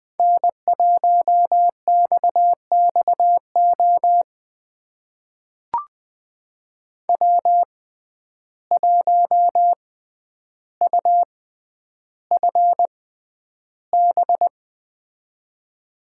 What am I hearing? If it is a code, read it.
Morse code, heard as a keyed tone, about 20 wpm, spaced as for 10 wpm, 700 Hz. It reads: N1XXO W1UFB
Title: USA Call Signs - 5 Characters - 6x Character Spacing - 20wpm